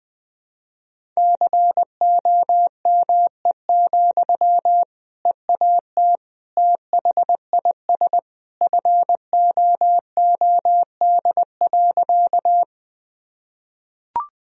Morse code, heard as a keyed tone, about 20 wpm, 700 Hz. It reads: COME, EAT THIS FOOD.
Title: Sentences from Top 1000 Words - Review Part 1 - 20wpm